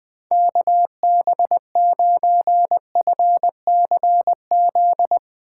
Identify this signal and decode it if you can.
Morse code, heard as a keyed tone, about 21 wpm, 700 Hz. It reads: KB9FCZ
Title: Speed Racing - USA Call Signs 20wpm